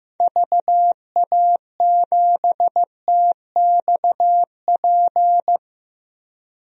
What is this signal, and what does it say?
Morse code, heard as a keyed tone, about 15 wpm, 700 Hz. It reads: VA7TXP
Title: DXCC-100 Call Signs - 6 Characters - Repeated 2x - 15wpm